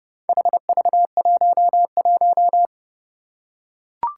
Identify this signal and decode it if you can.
Morse code, heard as a keyed tone, about 31 wpm, 700 Hz. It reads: HV11
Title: Four Challenging Letter-Number Combinations 30wpm